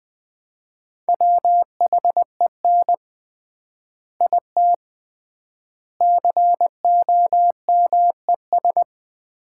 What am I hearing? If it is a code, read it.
Morse code, heard as a keyed tone, about 20 wpm, 700 Hz. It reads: WHEN IT COMES
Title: 3x Word-Spacing - Sets of 3 Words - 20wpm